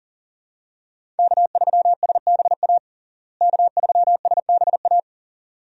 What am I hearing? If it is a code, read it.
Morse code, heard as a keyed tone, about 40 wpm, 700 Hz. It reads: K3SBA K3SBA
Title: USA Call Signs - 5 Characters - Repeated 2x - 40wpm